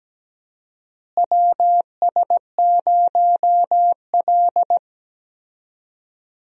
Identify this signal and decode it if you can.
Morse code, heard as a keyed tone, about 17 wpm, 700 Hz. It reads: WS0L
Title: USA Call Signs - 4 Characters - Repeated 3x - 17wpm